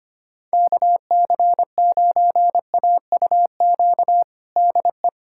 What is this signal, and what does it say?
Morse code, heard as a keyed tone, about 25 wpm, 700 Hz. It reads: KC9AUQ DE
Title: USA Callsign DE Callsign - 6 Characters - 25wpm